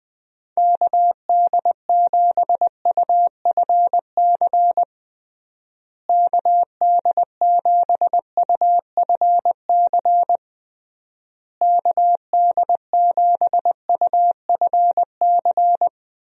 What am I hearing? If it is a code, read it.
Morse code, heard as a keyed tone, about 20 wpm, 700 Hz. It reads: KD7UFC KD7UFC KD7UFC
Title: USA Call Signs - 6 Characters - Repeated 3x - 20wpm